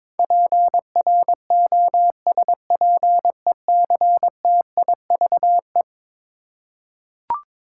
Morse code, heard as a keyed tone, about 22 wpm, 700 Hz. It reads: PROSPECTIVE
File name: practice / 22wpm / Eleven Letter Words - 22wpm